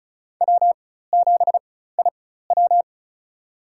Morse code, heard as a keyed tone, about 36 wpm, spaced as for 16 wpm, 700 Hz. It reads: W7IW
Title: USA Call Signs - 4 Characters - 4x Character Spacing - 35wpm